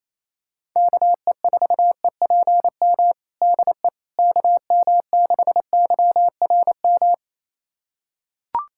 Morse code, heard as a keyed tone, about 28 wpm, 700 Hz. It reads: KE4EPM DE KM6YRM
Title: USA Callsign DE Callsign - 6 Characters - 28wpm